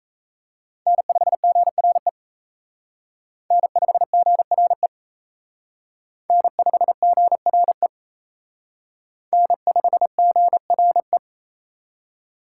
Morse code, heard as a keyed tone, about 34 wpm, 700 Hz. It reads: N5GRE N5GRE N5GRE N5GRE
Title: Speed Racing - USA Call Signs 28wpm